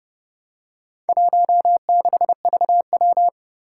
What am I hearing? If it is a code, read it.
Morse code, heard as a keyed tone, about 30 wpm, 700 Hz. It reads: 16VW
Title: Four Challenging Letter-Number Combinations 30wpm